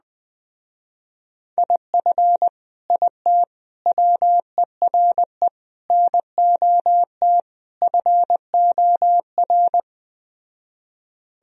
Morse code, heard as a keyed tone, about 20 wpm, 700 Hz. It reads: IF IT WERE NOT FOR
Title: Speed-Racing - Sets of 5 Words 17wpm